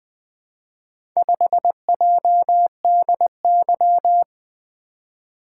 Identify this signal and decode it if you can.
Morse code, heard as a keyed tone, about 20 wpm, 700 Hz. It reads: 5JDY
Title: Four Letter-Number Combinations - 20wpm